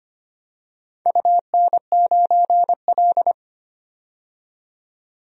Morse code, heard as a keyed tone, about 25 wpm, 700 Hz. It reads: UN9L
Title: DXCC-100 Call Signs - 4 Characters - 25wpm